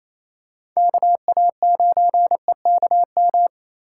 Morse code, heard as a keyed tone, about 27 wpm, 700 Hz. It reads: KA9EKM
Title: USA Call Signs - 3x Character Spacing - 28wpm